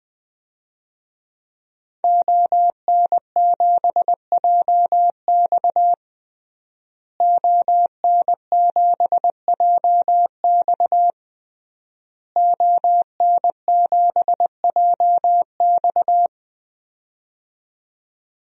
Morse code, heard as a keyed tone, about 20 wpm, 700 Hz. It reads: ON7JX ON7JX ON7JX
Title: DXCC-100 Call Signs - 5 Characters - Repeated 3x - 20wpm